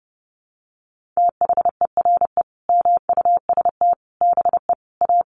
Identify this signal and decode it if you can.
Morse code, heard as a keyed tone, about 28 wpm, 700 Hz. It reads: THERE MUST BE A
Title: Sentences from Top 1400 Words 30wpm